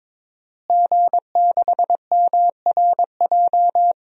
Morse code, heard as a keyed tone, about 21 wpm, 700 Hz. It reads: G6MRJ
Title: DXCC-100 Call Signs - 5 Characters - Repeated 3x - 22wpm